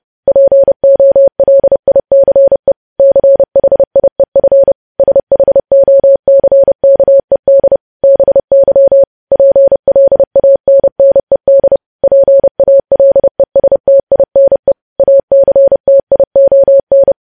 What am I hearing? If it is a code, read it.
Morse code, heard as a keyed tone, about 30 wpm, 550 Hz. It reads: POLICE CHIEF SHOCKED BY PLANNED PALESTINE ACTION
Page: news headlines in morse code at 30 wpm